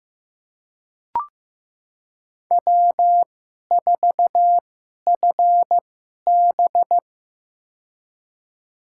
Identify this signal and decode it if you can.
Morse code, heard as a keyed tone, about 15 wpm, spaced as for 11 wpm, 700 Hz. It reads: W4FB
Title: USA Call Signs - 4 Characters - 2x Character Spacing - 15wpm